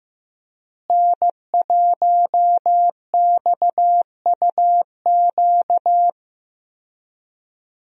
Morse code, heard as a keyed tone, about 15 wpm, 700 Hz. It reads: N1XUQ
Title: USA Call Signs - 5 Characters - Repeated 3x - 15wpm